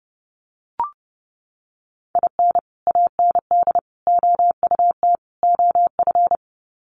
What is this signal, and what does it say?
Morse code, heard as a keyed tone, about 30 wpm, 700 Hz. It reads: IN AND OUT OF
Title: Sets of 4 Words 30wpm